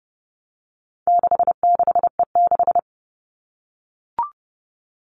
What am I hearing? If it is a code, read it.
Morse code, heard as a keyed tone, about 30 wpm, 700 Hz. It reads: 66E6
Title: All Cut Numbers - 4-Digit 30wpm